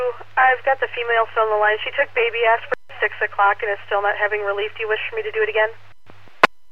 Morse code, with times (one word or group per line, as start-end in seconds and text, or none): none